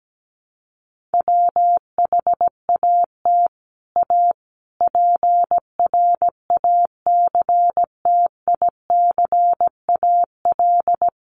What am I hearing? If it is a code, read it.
Morse code, heard as a keyed tone, about 17 wpm, 700 Hz. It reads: WHAT A PRACTICAL
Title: Sentences from Top 1400 Words 17wpm